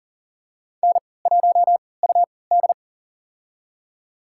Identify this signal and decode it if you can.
Morse code, heard as a keyed tone, about 40 wpm, spaced as for 23 wpm, 700 Hz. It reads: N1UD
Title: USA Call Signs - 4 Characters - 3x Character Spacing - 40wpm